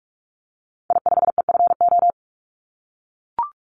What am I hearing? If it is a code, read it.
Morse code, heard as a keyed tone, about 44 wpm, 700 Hz. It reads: I5EFO
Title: Call Signs - 45wpm